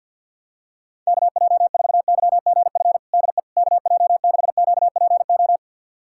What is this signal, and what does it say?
Morse code, heard as a keyed tone, about 52 wpm, 700 Hz. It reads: KJ4YGF DE KJ6QPO
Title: USA Callsign DE Callsign 50wpm